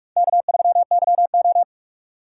Morse code, heard as a keyed tone, about 42 wpm, 700 Hz. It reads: K3YO